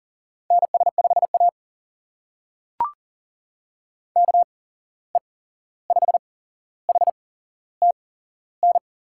Morse code, heard as a keyed tone, about 42 wpm, spaced as for 15 wpm, 700 Hz. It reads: NS5A KE5HTN
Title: USA Call Signs - 8x Character Spacing - 40wpm